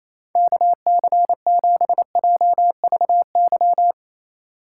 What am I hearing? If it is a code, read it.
Morse code, heard as a keyed tone, about 35 wpm, 700 Hz. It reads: KC7JVY
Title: Speed Racing - USA 2x3 Call Signs 28wpm